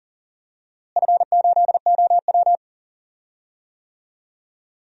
Morse code, heard as a keyed tone, about 37 wpm, 700 Hz. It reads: F8OW
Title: DXCC-100 Call Signs - 4 Characters - Repeated 3x - 40wpm